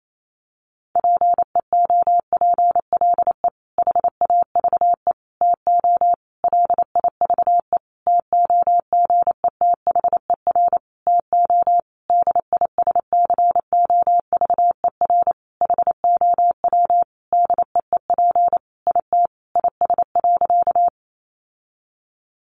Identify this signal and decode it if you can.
Morse code, heard as a keyed tone, about 28 wpm, 700 Hz. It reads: PEOPLE HAVE TO LIVE TOGETHER TO DISCOVER HOW DEEP IT IS.
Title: Sentences from Top 1100 Words 28wpm